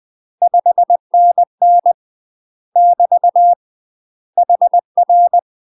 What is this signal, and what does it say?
Morse code, heard as a keyed tone, about 20 wpm, 700 Hz. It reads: 5NN <BT> HR
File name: UK MASTER QSOs TRAINING / 20WPM / UK MASTER QSOs 0067